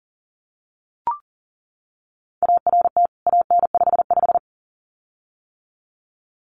Morse code, heard as a keyed tone, about 40 wpm, 700 Hz. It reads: ART AN55
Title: CWT Exchange 40wpm